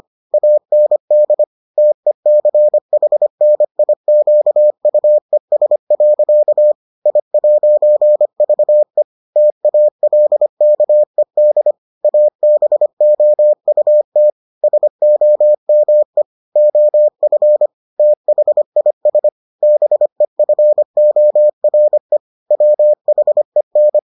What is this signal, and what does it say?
Morse code, heard as a keyed tone, about 25 wpm, 600 Hz. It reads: AND TECHNIQUES. I'VE TALKED ABOUT SOME OF THIS BEFORE WHEN